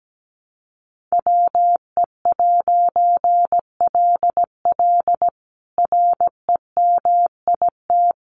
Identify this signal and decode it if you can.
Morse code, heard as a keyed tone, about 17 wpm, 700 Hz. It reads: WE'LL REMIT
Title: A Sound Of Thunder 17wpm